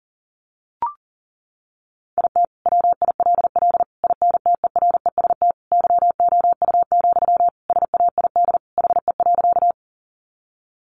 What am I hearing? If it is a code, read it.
Morse code, heard as a keyed tone, about 37 wpm, 700 Hz. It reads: IT WILL INTEREST YOU, SAID HE.
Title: Sentences from Top 400 Words 40wpm